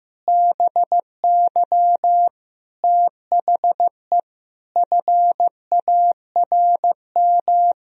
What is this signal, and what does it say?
Morse code, heard as a keyed tone, about 15 wpm, 700 Hz. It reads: BY THE FARM
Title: Sentences from Top 1000 Words - Review Part 2 - 15wpm